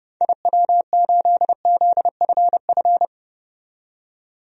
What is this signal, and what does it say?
Morse code, heard as a keyed tone, about 29 wpm, 700 Hz. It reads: IW8ZFF
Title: DXCC-100 Call Signs - 6 Characters - 30wpm